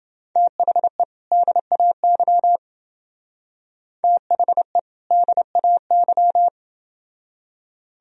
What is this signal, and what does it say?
Morse code, heard as a keyed tone, about 28 wpm, 700 Hz. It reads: THE DAY THE DAY